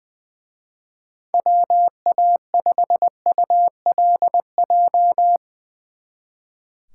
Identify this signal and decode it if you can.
Morse code, heard as a keyed tone, about 21 wpm, 700 Hz. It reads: WA5ULJ
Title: Speed Racing - USA 2x3 Call Signs 20wpm